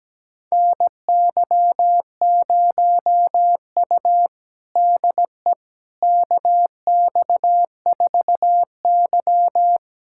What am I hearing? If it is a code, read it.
Morse code, heard as a keyed tone, about 17 wpm, 700 Hz. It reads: NY0U DE KX4Y
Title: USA Callsign DE Callsign - 4 Characters - 17wpm